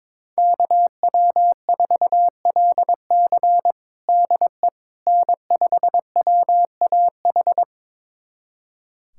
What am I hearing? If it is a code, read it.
Morse code, heard as a keyed tone, about 22 wpm, 700 Hz. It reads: KW4LC DE N5WAH
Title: USA Callsign DE Callsign - 5 Characters - 22wpm